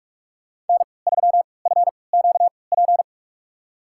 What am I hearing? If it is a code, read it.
Morse code, heard as a keyed tone, about 41 wpm, spaced as for 25 wpm, 700 Hz. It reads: N3FQP
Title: USA Call Signs - 5 Characters - 3x Character Spacing - 45wpm